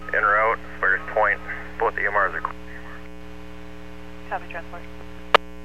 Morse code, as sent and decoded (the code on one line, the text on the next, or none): none